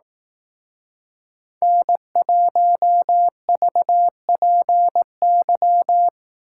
Morse code, heard as a keyed tone, about 18 wpm, 700 Hz. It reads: N1VPY
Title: Speed Racing - USA 1x3 Call Signs 15wpm